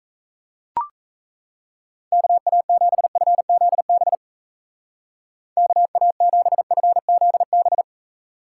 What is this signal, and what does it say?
Morse code, heard as a keyed tone, about 36 wpm, 700 Hz. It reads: KA7FZB KA7FZB